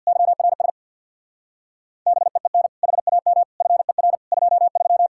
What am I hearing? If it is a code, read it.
Morse code, heard as a keyed tone, about 66 wpm, 700 Hz. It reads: YRS BEEN HAM FER 23